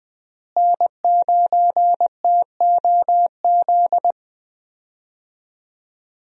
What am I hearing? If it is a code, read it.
Morse code, heard as a keyed tone, about 20 wpm, 700 Hz. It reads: N9TOZ
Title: USA Call Signs - 5 Characters - Repeated 3x - 20wpm